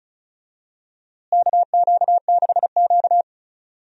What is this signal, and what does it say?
Morse code, heard as a keyed tone, about 36 wpm, 700 Hz. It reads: KQ6Q
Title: USA Call Signs - 4 Characters - Repeated 3x - 35wpm